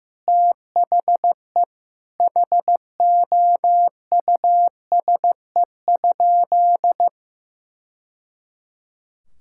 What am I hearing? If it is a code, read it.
Morse code, heard as a keyed tone, about 15 wpm, 700 Hz. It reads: THE HOUSE?
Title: Sentences from Top 1000 Words - Review Part 5 - 15wpm